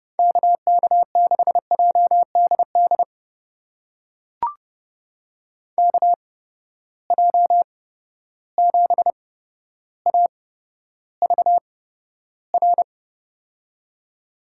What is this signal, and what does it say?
Morse code, heard as a keyed tone, about 30 wpm, spaced as for 13 wpm, 700 Hz. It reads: KK6JDD KJ7AVR
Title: USA Call Signs - 6 Characters - 8x Character Spacing - 30wpm